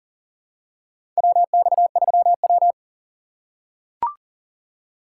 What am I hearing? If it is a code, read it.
Morse code, heard as a keyed tone, about 40 wpm, 700 Hz. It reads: WX3W
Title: USA Call Signs - 4 Characters - 3x Character Spacing - 40wpm